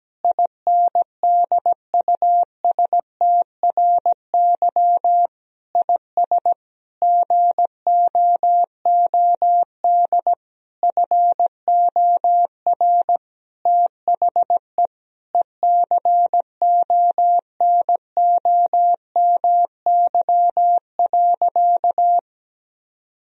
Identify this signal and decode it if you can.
Morse code, heard as a keyed tone, about 17 wpm, 700 Hz. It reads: INDUSTRY IS GOOD FOR THE ECONOMY.
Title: Sentences from Top 1000 Words - Review Part 4 - 17wpm